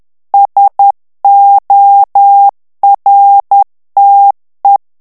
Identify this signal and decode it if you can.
Morse code, heard as a keyed tone, about 10 wpm, 800 Hz. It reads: SORTE
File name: morse mp3 / FRANCAIS TEXTE CLAIR 2 VITESSE 10 MOTS MINUTES